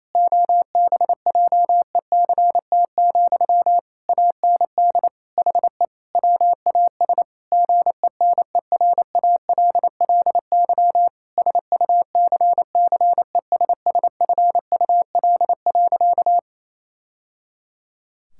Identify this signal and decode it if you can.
Morse code, heard as a keyed tone, about 28 wpm, 700 Hz. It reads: OBJECT, AND HE WAS GENERALLY SUCCESSFUL.
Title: Sentences from Top 1000 Words - Review Part 6 - 28wpm